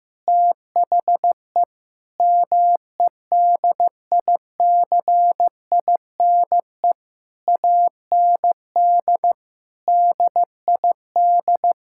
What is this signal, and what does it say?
Morse code, heard as a keyed tone, about 15 wpm, 700 Hz. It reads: THE MEDICINE AND DID
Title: Sentences from Top 1000 Words - Review Part 6 - 15wpm